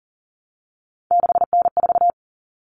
Morse code, heard as a keyed tone, about 40 wpm, 700 Hz. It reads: 6N4
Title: Common Cut Numbers - 3-Digit 40wpm